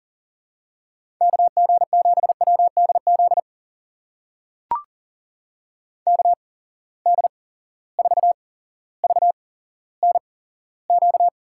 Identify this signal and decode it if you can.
Morse code, heard as a keyed tone, about 40 wpm, spaced as for 17 wpm, 700 Hz. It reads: KG7WDZ KD4VNQ
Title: USA Call Signs - 6 Characters - 8x Character Spacing - 40wpm